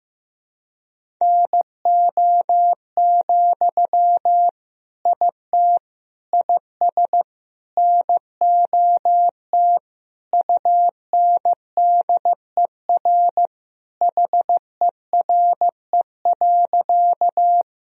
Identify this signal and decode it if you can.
Morse code, heard as a keyed tone, about 15 wpm, 700 Hz. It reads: NO, IT IS NOT UNDER HERE.
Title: Sentences from Top 1000 Words - Review Part 1 - 15wpm